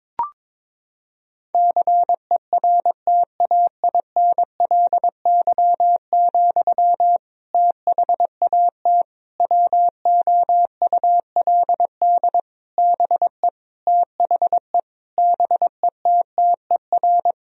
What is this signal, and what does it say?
Morse code, heard as a keyed tone, about 22 wpm, 700 Hz. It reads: CERTAINLY, THAT WOULD BE THE BETTER
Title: Sentences from Top 1000 Words - Review Part 2 - 22wpm